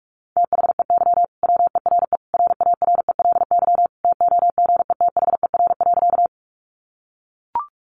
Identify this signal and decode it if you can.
Morse code, heard as a keyed tone, about 44 wpm, 700 Hz. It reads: THEY WERE RARELY TOGETHER.
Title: Sentences from Top 1400 Words 45wpm